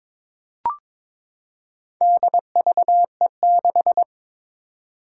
Morse code, heard as a keyed tone, about 23 wpm, 700 Hz. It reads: DVE6
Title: Four Challenging Letter-Number Combinations 22wpm